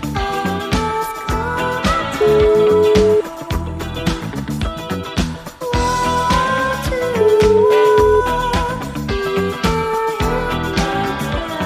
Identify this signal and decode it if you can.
Ringback tone (Europe, 425 Hz), with noise.